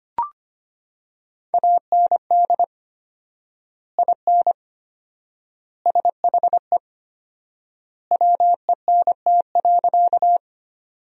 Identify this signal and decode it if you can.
Morse code, heard as a keyed tone, about 25 wpm, 700 Hz. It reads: AND IN SHE WENT.
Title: 4x Word-Spacing - Sentences from Top 200 Words - 25wpm